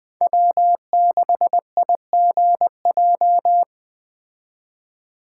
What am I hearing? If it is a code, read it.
Morse code, heard as a keyed tone, about 20 wpm, 700 Hz. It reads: W6IGJ
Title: Speed Racing - USA 1x3 Call Signs 17wpm